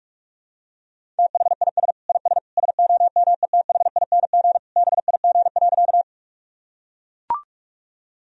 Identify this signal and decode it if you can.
Morse code, heard as a keyed tone, about 44 wpm, 700 Hz. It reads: THIS IS SOMETHING BIG.
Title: Sentences from Top 1000 Words - Review Part 1 - 45wpm